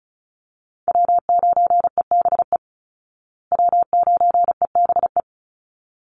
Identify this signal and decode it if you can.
Morse code, heard as a keyed tone, about 34 wpm, 700 Hz. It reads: W9EBE W9EBE
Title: Call Signs Repeated 3x - 35wpm